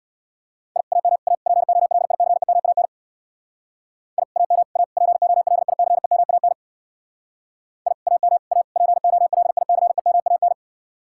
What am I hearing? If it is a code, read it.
Morse code, heard as a keyed tone, about 65 wpm, 700 Hz. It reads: I AM A POLICEMAN I AM A POLICEMAN I AM A POLICEMAN